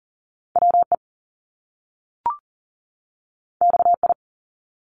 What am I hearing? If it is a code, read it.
Morse code, heard as a keyed tone, about 38 wpm, 700 Hz. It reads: WE XI
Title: Top Two Letter Combinations in Words 40wpm